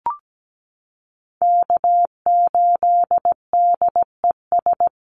Morse code, heard as a keyed tone, about 17 wpm, 700 Hz. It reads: K8DES
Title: USA Call Signs Repeated 3x 17wpm